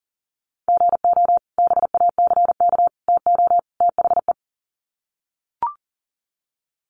Morse code, heard as a keyed tone, about 40 wpm, 700 Hz. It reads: GO BACK TO THE
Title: Sets of 4 Words 40wpm